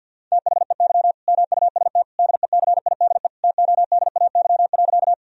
Morse code, heard as a keyed tone, about 50 wpm, 700 Hz. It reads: THEY MUST DECIDE TODAY.
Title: Sentences from Top 1000 Words - Review Part 3 - 50wpm